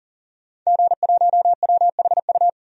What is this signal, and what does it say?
Morse code, heard as a keyed tone, about 40 wpm, 700 Hz. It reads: G1WHU
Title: DXCC-100 Call Signs - 5 Characters - 40wpm